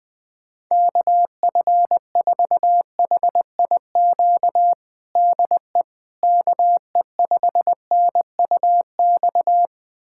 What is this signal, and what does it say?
Morse code, heard as a keyed tone, about 20 wpm, 700 Hz. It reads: KF4HIQ DE KE5NUX
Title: USA Callsign DE Callsign 20wpm